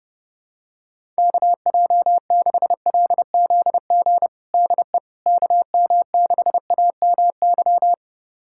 Morse code, heard as a keyed tone, about 30 wpm, 700 Hz. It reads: KJ6LZG DE KM6AMY
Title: USA Callsign DE Callsign - 6 Characters - 30wpm